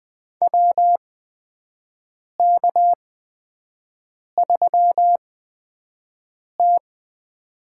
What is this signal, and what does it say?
Morse code, heard as a keyed tone, about 20 wpm, spaced as for 5 wpm, 700 Hz. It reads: WK3T
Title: USA Call Signs - 4 Characters - 8x Character Spacing - 20wpm